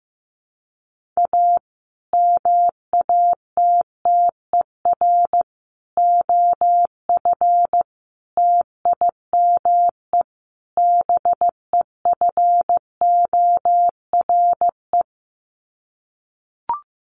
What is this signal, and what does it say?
Morse code, heard as a keyed tone, about 15 wpm, 700 Hz. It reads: A MATTER OF TIME BEFORE
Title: Sets of 5 Words Encore 15wpm